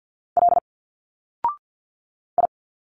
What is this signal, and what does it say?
Morse code, heard as a keyed tone, about 49 wpm, 700 Hz. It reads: L I
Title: Single Letters 50wpm